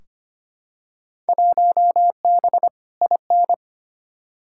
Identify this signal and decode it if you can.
Morse code, heard as a keyed tone, about 25 wpm, 700 Hz. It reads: 1B IN
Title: Field Day Exchanges - 25wpm